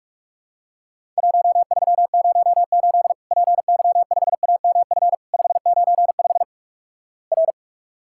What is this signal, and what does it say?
Morse code, heard as a keyed tone, about 46 wpm, 700 Hz. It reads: 1308 PY5AMF 595 R
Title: Worked All Europe DX Contest - QTC Exchange - 45wpm